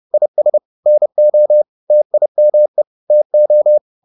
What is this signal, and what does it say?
Morse code, heard as a keyed tone, about 30 wpm, 600 Hz. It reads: IS NO TIME TO
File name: the old man and the sea / the old man and the sea 1 30wpm